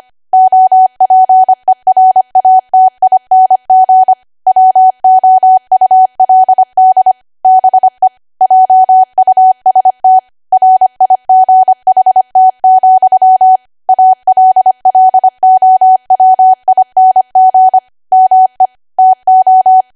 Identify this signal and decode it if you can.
Morse code, heard as a keyed tone, about 25 wpm, 750 Hz. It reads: OPERATING WOULD BE JUST RIGHT, ALLOWING ME TO
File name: Archive / 25 WPM / 25WPM